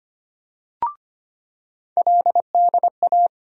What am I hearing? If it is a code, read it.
Morse code, heard as a keyed tone, about 25 wpm, 700 Hz. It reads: LDA
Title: USA Call Sign Postfixes - 3 Letters 25wpm